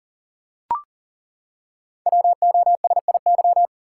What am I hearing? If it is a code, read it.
Morse code, heard as a keyed tone, about 38 wpm, 690 Hz. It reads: WOSIY
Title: Code Groups - 5 Letter:Numbers - 40wpm